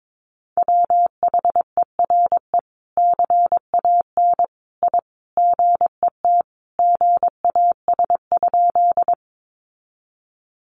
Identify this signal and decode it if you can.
Morse code, heard as a keyed tone, about 22 wpm, 700 Hz. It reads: WHERE CAN I GET GAS?
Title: Sentences from Top 1100 Words 22wpm